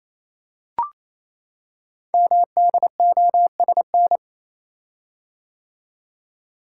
Morse code, heard as a keyed tone, about 29 wpm, 700 Hz. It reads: MDOSN
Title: Code Groups - 5 Letter:Numbers - 28wpm